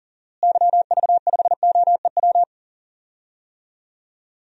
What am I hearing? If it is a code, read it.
Morse code, heard as a keyed tone, about 40 wpm, 700 Hz. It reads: YV5OEW
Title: DXCC-100 Call Signs - 6 Characters - Repeated 2x - 40wpm